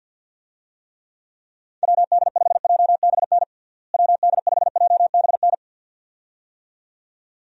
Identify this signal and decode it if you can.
Morse code, heard as a keyed tone, about 52 wpm, 700 Hz. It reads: WD5JBN WD5JBN
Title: USA Call Signs - 6 Characters - Repeated 2x - 50wpm